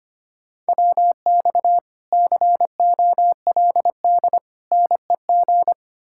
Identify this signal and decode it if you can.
Morse code, heard as a keyed tone, about 33 wpm, 700 Hz. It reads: WX COLD NEG